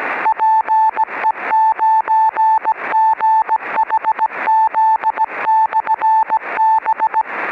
Morse code, heard as a keyed tone, about 17 wpm, 880 Hz. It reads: PE9GHZ/B
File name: 10GHz-PLL VCO-beacon